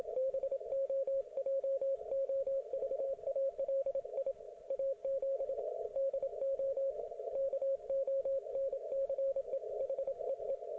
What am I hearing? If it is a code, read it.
Morse code, heard as a keyed tone, about 27 wpm, 540 Hz. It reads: BOJOVALI AZ DO KONCE HEE